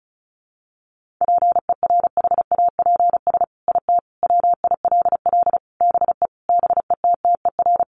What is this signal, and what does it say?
Morse code, heard as a keyed tone, about 34 wpm, 700 Hz. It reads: PERHAPS IT WILL BE BETTER
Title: Sentences from Top 1500 Words 35wpm